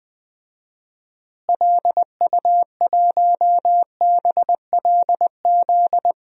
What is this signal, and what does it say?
Morse code, heard as a keyed tone, about 19 wpm, 700 Hz. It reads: LU1BLZ
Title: DXCC-100 Call Signs - 6 Characters - Repeated 3x - 20wpm